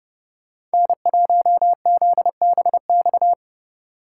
Morse code, heard as a keyed tone, about 29 wpm, 700 Hz. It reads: N1ZBX